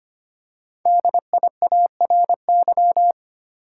Morse code, heard as a keyed tone, about 25 wpm, 700 Hz. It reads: DIARY
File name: 5 Letter Words - First 3 Letters Given - 25wpm